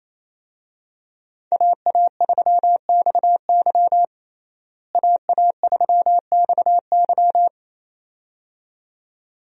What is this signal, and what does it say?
Morse code, heard as a keyed tone, about 28 wpm, 700 Hz. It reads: AA3XY AA3XY
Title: USA Call Signs - 5 Characters - Repeated 2x - 28wpm